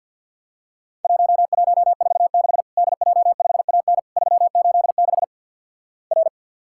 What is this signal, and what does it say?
Morse code, heard as a keyed tone, about 49 wpm, 700 Hz. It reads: 1146 DJ5AN 386 R
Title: Worked All Europe DX Contest - QTC Exchange - 50wpm